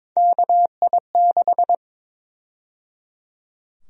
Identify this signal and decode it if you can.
Morse code, heard as a keyed tone, about 22 wpm, 700 Hz. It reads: KI6
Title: USA Call Sign Prefixes - 22wpm